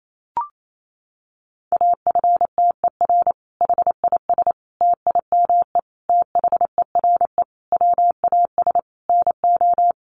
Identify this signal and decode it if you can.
Morse code, heard as a keyed tone, about 28 wpm, 700 Hz. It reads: AFTER HIS TIME THERE WAS NO
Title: Sentences from Top 1400 Words 28wpm